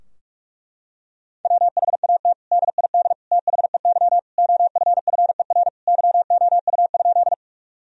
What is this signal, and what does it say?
Morse code, heard as a keyed tone, about 46 wpm, 700 Hz. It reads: WHAT DID THEY OFFER YOU?
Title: Sentences from Top 1000 Words - Review Part 4 - 45wpm